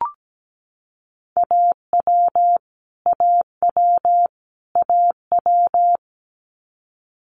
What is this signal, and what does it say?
Morse code, heard as a keyed tone, about 17 wpm, 700 Hz. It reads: AW AW AW
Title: Multi-letter Phonemes Repeated 3x 17wpm